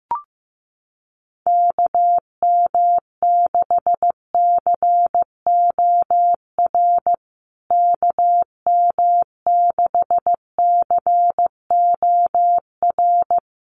Morse code, heard as a keyed tone, about 15 wpm, 700 Hz. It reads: KM6COR KM6COR
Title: USA Call Signs Repeated 3x 15wpm